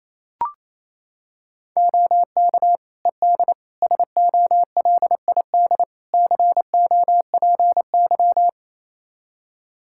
Morse code, heard as a keyed tone, about 28 wpm, 700 Hz. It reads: OK ED SOLID COPY
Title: Practice QSO - Exchange 3 - More Info - 28wpm